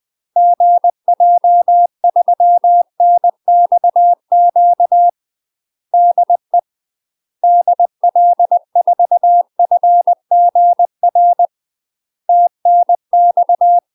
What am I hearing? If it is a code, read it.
Morse code, heard as a keyed tone, about 20 wpm, 700 Hz. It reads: GJ3NXQ DE DL4FGR TNX
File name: UK MASTER QSOs TRAINING / 20WPM / UK MASTER QSOs 0002